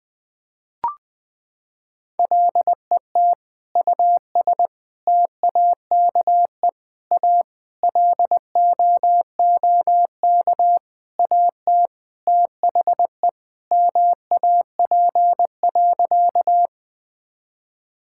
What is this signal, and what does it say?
Morse code, heard as a keyed tone, about 20 wpm, 700 Hz. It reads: LET US TAKE A LOOK AT THE MAP.
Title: Sentences with 5-Word Phrases - 20wpm